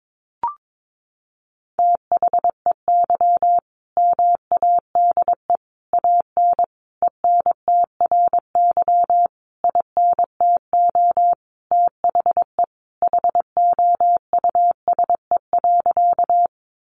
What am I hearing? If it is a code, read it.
Morse code, heard as a keyed tone, about 22 wpm, 700 Hz. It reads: THEY MADE AN ENTRY INTO THE HOUSE.
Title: Sentences from Top 1500 Words 22wpm